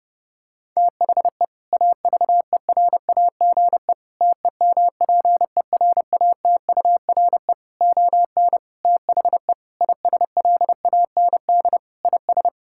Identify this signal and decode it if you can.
Morse code, heard as a keyed tone, about 30 wpm, 700 Hz. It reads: THE AVERAGE TEMPERATURE ON THE ISLAND IS
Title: Sentences from Top 1000 Words - Review Part 6 - 30wpm